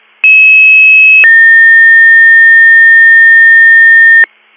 Two-tone page, heard as a one-second tone, then a three-second tone, about 2600 then 1800 Hz, with noise. It is heard from 0.2 to 4.2 s.